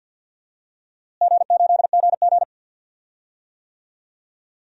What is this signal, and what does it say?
Morse code, heard as a keyed tone, about 51 wpm, 700 Hz. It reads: G8GG
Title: DXCC-100 Call Signs - 4 Characters - Repeated 2x - 50wpm